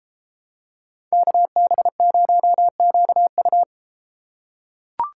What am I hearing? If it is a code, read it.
Morse code, heard as a keyed tone, about 31 wpm, 700 Hz. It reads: KB0QU